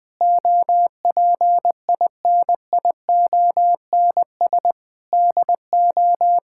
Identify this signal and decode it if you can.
Morse code, heard as a keyed tone, about 21 wpm, 700 Hz. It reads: OPINIONS DO
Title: Sentences with 8-Word Phrases - Spoken Beginning - 20wpm